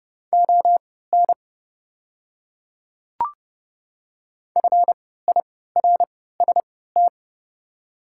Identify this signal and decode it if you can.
Morse code, heard as a keyed tone, about 30 wpm, spaced as for 17 wpm, 690 Hz. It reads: ON FIRST
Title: 3x Character Spacing - Top 200 Words - 30wpm